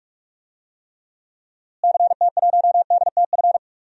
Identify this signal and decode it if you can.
Morse code, heard as a keyed tone, about 46 wpm, 700 Hz. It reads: CT1DTF
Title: DXCC-100 Call Signs - 6 Characters - Repeated 2x - 45wpm